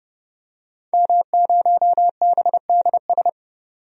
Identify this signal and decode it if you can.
Morse code, heard as a keyed tone, about 32 wpm, 700 Hz. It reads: M0BDS
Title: DXCC-100 Call Signs - 5 Characters - 30wpm